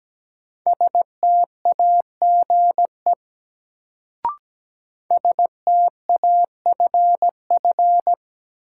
Morse code, heard as a keyed tone, about 17 wpm, 700 Hz. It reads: STAGE STAFF
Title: Sentences from Top 1200 Words - New Word Drill Alt - 17wpm